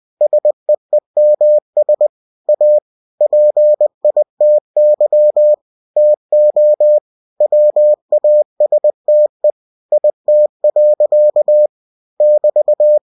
Morse code, heard as a keyed tone, about 20 wpm, 600 Hz. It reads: SEEMS A PITY TO WASTE IT. <BT>